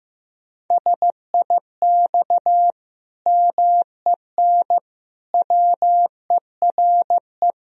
Morse code, heard as a keyed tone, about 15 wpm, 700 Hz. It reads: SIX MEN WERE
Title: Sentences from Top 1000 Words - Review Part 2 - 15wpm